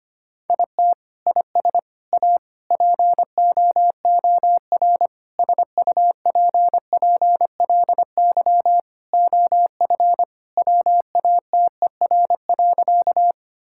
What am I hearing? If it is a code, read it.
Morse code, heard as a keyed tone, about 25 wpm, 700 Hz. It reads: IT IS A POOR SUPPLY OF WATER.
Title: Sentences from Top 1000 Words - Review Part 5 - 25wpm